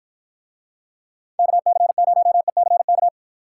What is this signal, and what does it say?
Morse code, heard as a keyed tone, about 52 wpm, 700 Hz. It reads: KC0ECK